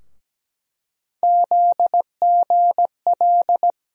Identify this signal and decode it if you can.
Morse code, heard as a keyed tone, about 17 wpm, 700 Hz. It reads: ZGL
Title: USA Call Sign Postfixes - 3 Letters 17wpm